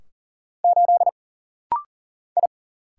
Morse code, heard as a keyed tone, about 40 wpm, 700 Hz. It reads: 8 I
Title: Single Letter-Number - Rapid-Fire 40wpm